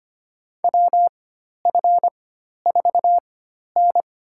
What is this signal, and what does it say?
Morse code, heard as a keyed tone, about 25 wpm, spaced as for 12 wpm, 700 Hz. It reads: WF4N
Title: USA Call Signs - 4 Characters - 4x Character Spacing - 25wpm